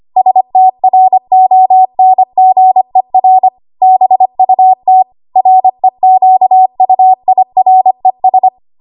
Morse code, heard as a keyed tone, about 25 wpm, 750 Hz. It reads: STRONGER BUT REQUIRES
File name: Archive / 25 WPM / 25WPM